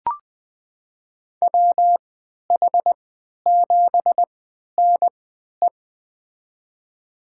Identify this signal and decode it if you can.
Morse code, heard as a keyed tone, about 20 wpm, spaced as for 11 wpm, 700 Hz. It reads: WH7NE
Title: USA Call Signs - 5 Characters - 3x Character Spacing - 20wpm